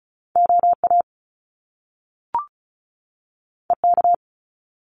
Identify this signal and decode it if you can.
Morse code, heard as a keyed tone, about 36 wpm, 700 Hz. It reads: OA EK
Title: Top Two Letter Combinations in Words 35wpm